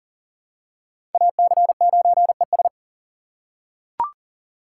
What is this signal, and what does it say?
Morse code, heard as a keyed tone, about 38 wpm, 690 Hz. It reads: AC9ES
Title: USA Call Signs - 4x Character Spacing - 40wpm